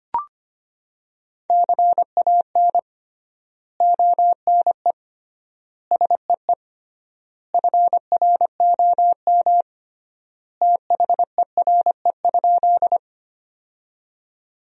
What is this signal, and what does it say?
Morse code, heard as a keyed tone, about 25 wpm, 700 Hz. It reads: CAN ONE SEE FROM THERE?
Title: 3x Word-Spacing - Sentences from Top 100 Words - 25wpm